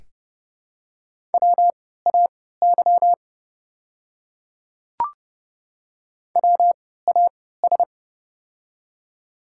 Morse code, heard as a keyed tone, about 30 wpm, spaced as for 17 wpm, 700 Hz. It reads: WAY WAS
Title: 3x Character Spacing - Top 100 Words - 30wpm